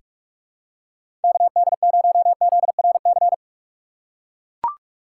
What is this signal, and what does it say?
Morse code, heard as a keyed tone, about 44 wpm, 700 Hz. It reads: KD0ZRC
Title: USA Call Signs - 6 Characters - 6x Character Spacing - 45wpm